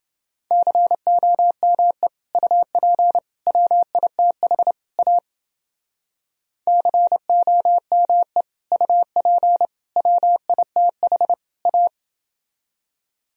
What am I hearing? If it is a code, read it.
Morse code, heard as a keyed tone, about 29 wpm, 700 Hz. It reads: COME UP WITH A COME UP WITH A